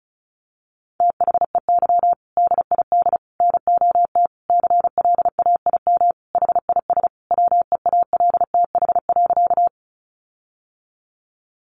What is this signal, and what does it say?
Morse code, heard as a keyed tone, about 34 wpm, 700 Hz. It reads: THEY DID NOT CLAIM HIS WEALTH.
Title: Sentences from Top 1500 Words 35wpm